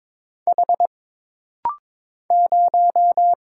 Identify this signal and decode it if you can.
Morse code, heard as a keyed tone, about 22 wpm, 700 Hz. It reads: H 0
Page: This rapid-fire timing is geared to push your ICR (Instant Character Recognition) skills.